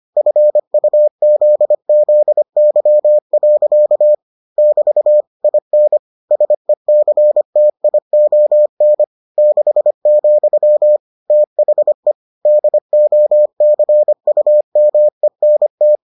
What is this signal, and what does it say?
Morse code, heard as a keyed tone, about 25 wpm, 600 Hz. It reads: FUZZY. <BT> IN SECTION 6, THE DOCUMENT